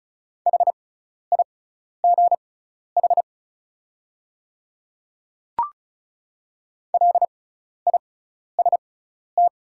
Morse code, heard as a keyed tone, about 36 wpm, spaced as for 12 wpm, 700 Hz. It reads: HIGH LIST
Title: 6x Character Spacing - Top 500 Words - 35wpm